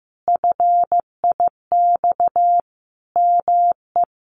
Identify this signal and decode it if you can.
Morse code, heard as a keyed tone, about 15 wpm, 700 Hz. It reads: FIX ME
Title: Sentences from Top 1400 Words 15wpm